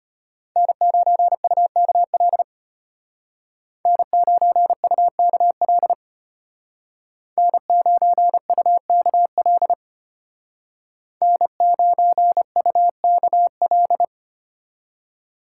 Morse code, heard as a keyed tone, about 33 wpm, 700 Hz. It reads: N9UKL N9UKL N9UKL N9UKL